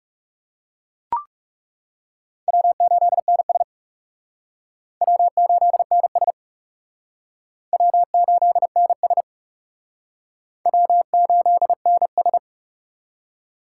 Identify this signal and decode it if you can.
Morse code, heard as a keyed tone, about 38 wpm, 700 Hz. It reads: W8NS W8NS W8NS W8NS